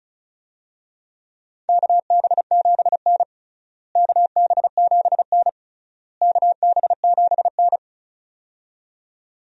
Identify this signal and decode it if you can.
Morse code, heard as a keyed tone, about 36 wpm, 700 Hz. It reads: KB7N KB7N KB7N
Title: USA Call Signs - 4 Characters - Repeated 3x - 35wpm